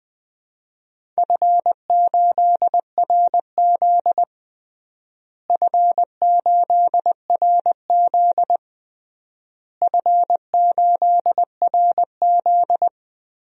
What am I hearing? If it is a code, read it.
Morse code, heard as a keyed tone, about 19 wpm, 700 Hz. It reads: F8RZ F8RZ F8RZ
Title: DXCC-100 Call Signs - 4 Characters - Repeated 3x - 20wpm